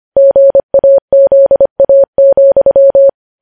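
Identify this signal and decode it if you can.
Morse code, heard as a keyed tone, about 25 wpm, 550 Hz. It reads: GAZA,
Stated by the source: Morse news headlines sent at 25 wpm.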